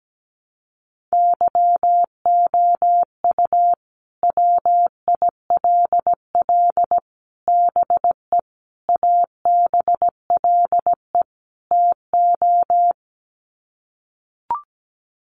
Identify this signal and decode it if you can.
Morse code, heard as a keyed tone, about 17 wpm, 700 Hz. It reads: YOU WILL BE ABLE TO
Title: Sets of 5 Words Encore2 17wpm